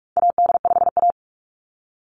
Morse code, heard as a keyed tone, about 44 wpm, 700 Hz. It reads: AD5A
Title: NA SOTA Chaser Callsigns 45wpm